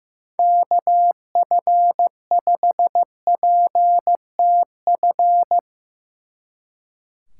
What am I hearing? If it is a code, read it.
Morse code, heard as a keyed tone, about 16 wpm, 700 Hz. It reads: KF5PTF
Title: Speed Racing - USA 2x3 Call Signs 15wpm